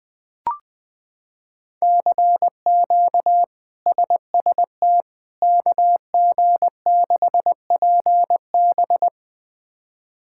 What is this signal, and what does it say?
Morse code, heard as a keyed tone, about 20 wpm, 700 Hz. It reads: CQ SST KG6PB
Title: K1USN SST QSO Practice - 20wpm